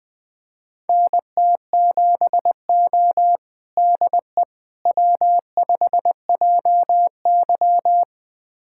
Morse code, heard as a keyed tone, about 20 wpm, 700 Hz. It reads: NT7O DE W5JY
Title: USA Callsign DE Callsign - 4 Characters - 20wpm